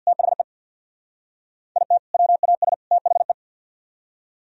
Morse code, heard as a keyed tone, about 63 wpm, 700 Hz. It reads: THE IT WAS THE